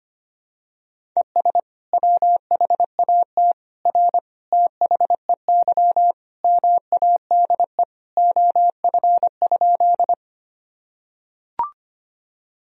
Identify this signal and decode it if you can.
Morse code, heard as a keyed tone, about 25 wpm, 700 Hz. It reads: ES WHAT R THEY MADE OF?
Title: Sentences from Top 200 Words and Abbreviations 25wpm